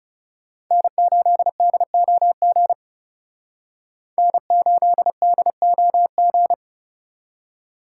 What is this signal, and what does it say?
Morse code, heard as a keyed tone, about 38 wpm, 700 Hz. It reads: N8DOG N8DOG